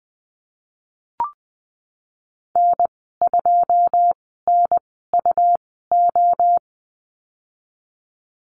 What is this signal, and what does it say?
Morse code, heard as a keyed tone, about 20 wpm, spaced as for 15 wpm, 700 Hz. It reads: N2NUO
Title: 186 Lesson 30g - Letter b - Call Signs 20wpm